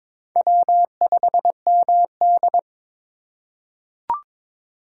Morse code, heard as a keyed tone, about 22 wpm, 700 Hz. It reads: W5MD
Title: USA Call Signs - 4 Characters - 2x Character Spacing - 22wpm